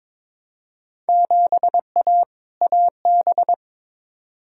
Morse code, heard as a keyed tone, about 21 wpm, 700 Hz. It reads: 7A AB
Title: Field Day Exchanges Encore - 22wpm